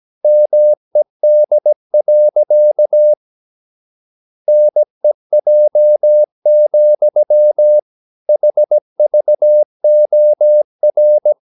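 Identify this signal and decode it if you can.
Morse code, heard as a keyed tone, about 17 wpm, 600 Hz. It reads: MED. NEJ, HVOR
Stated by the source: Slow (QRS) Morse practice at 17 wpm, from CW med Gnister.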